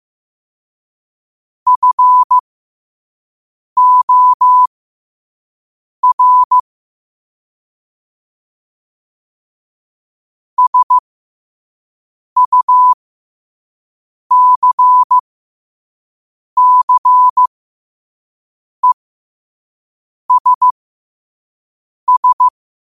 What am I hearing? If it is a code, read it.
Morse code, heard as a keyed tone, about 15 wpm, spaced as for 5 wpm, 1000 Hz. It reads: FOR SUCCESS